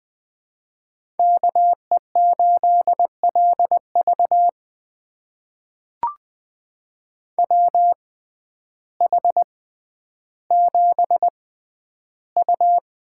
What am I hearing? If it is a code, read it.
Morse code, heard as a keyed tone, about 20 wpm, spaced as for 11 wpm, 700 Hz. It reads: KE8LV WH7U
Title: USA Call Signs - 5 Characters - 6x Character Spacing - 20wpm